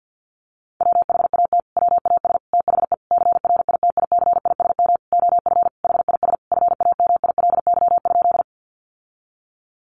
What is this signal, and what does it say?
Morse code, heard as a keyed tone, about 48 wpm, 700 Hz. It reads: WHAT WAS THE CRITICISM OF HIS FAMILY?
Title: Sentences from Top 1300 Words 50wpm